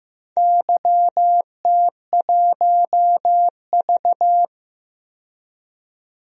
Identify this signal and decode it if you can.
Morse code, heard as a keyed tone, about 15 wpm, 700 Hz. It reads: YT1V
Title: DXCC-100 Call Signs - 4 Characters - 15wpm